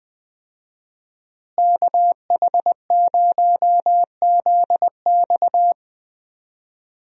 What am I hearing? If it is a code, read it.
Morse code, heard as a keyed tone, about 21 wpm, 700 Hz. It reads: KH0ZX
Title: USA Call Signs - 5 Characters - Repeated 3x - 20wpm